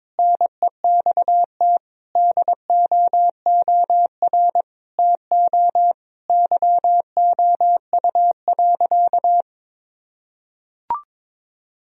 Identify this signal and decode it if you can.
Morse code, heard as a keyed tone, about 22 wpm, 700 Hz. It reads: NEXT DOOR TO YOU.
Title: Sentences from Top 1000 Words - Review Part 1 - 22wpm